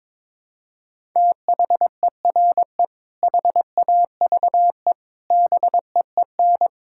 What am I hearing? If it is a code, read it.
Morse code, heard as a keyed tone, about 22 wpm, 700 Hz. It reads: THERE HAVE BEEN
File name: practice / 22wpm / Sets of 6 Words - 22wpm